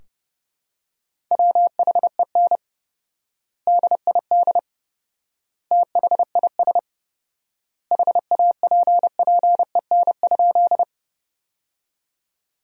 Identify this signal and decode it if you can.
Morse code, heard as a keyed tone, about 30 wpm, 700 Hz. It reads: WHEN DID THIS HAPPEN?
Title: 4x Word-Spacing - Sentences from Top 400 Words - 30wpm